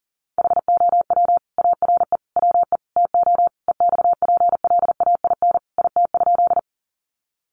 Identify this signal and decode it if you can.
Morse code, heard as a keyed tone, about 40 wpm, 700 Hz. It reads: HOW ARE WE TO EXPLAIN IT?
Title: Sentences from Top 1100 Words 40wpm